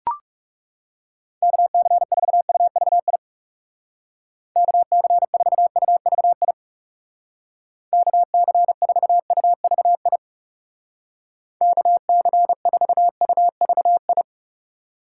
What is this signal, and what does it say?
Morse code, heard as a keyed tone, about 40 wpm, 700 Hz. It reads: KC4UVI KC4UVI KC4UVI KC4UVI